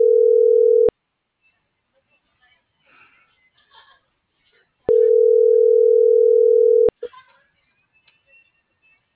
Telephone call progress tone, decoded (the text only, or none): ringback tone